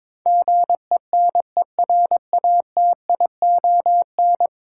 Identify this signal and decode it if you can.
Morse code, heard as a keyed tone, about 22 wpm, 700 Hz. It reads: GENERATION
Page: morse-code-ninja-practice-22wpm